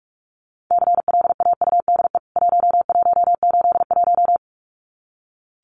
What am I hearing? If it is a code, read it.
Morse code, heard as a keyed tone, about 44 wpm, 700 Hz. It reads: CLAUDE 1181
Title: CWT Exchange 45wpm